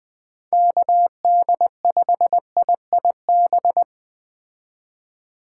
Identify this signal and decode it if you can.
Morse code, heard as a keyed tone, about 20 wpm, 700 Hz. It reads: KD5IIB
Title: Speed Racing - USA 2x3 Call Signs 17wpm